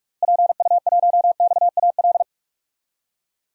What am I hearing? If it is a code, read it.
Morse code, heard as a keyed tone, about 46 wpm, 700 Hz. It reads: PU1XAL
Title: DXCC-100 Call Signs - 6 Characters - Repeated 3x - 45wpm